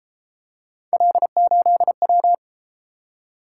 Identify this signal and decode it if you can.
Morse code, heard as a keyed tone, about 30 wpm, 700 Hz. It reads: L8W